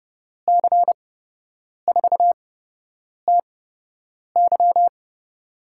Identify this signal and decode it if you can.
Morse code, heard as a keyed tone, about 32 wpm, spaced as for 8 wpm, 700 Hz. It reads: C4TY
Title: USA Call Signs - 8x Character Spacing - 30wpm